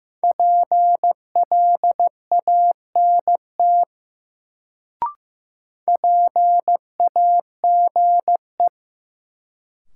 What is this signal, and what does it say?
Morse code, heard as a keyed tone, about 15 wpm, 700 Hz. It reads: PLANT PAGE
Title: Sentences from Top 1000 Words - Review Part 1 - New Word Drill - 15wpm